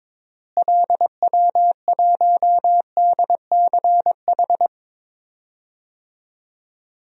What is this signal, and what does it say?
Morse code, heard as a keyed tone, about 22 wpm, 700 Hz. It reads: LW1DCH
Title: DXCC-100 Call Signs - 6 Characters - Repeated 2x - 22wpm